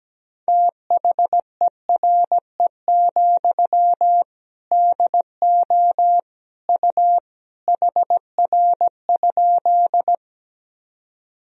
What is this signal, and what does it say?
Morse code, heard as a keyed tone, about 17 wpm, 700 Hz. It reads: THERE, DO U HR?
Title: Sentences from Top 300 Words and Abbreviations 17wpm